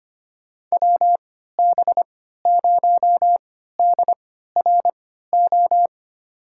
Morse code, heard as a keyed tone, about 25 wpm, spaced as for 14 wpm, 700 Hz. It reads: WB0DRO
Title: USA Call Signs - 6 Characters - 3x Character Spacing - 25wpm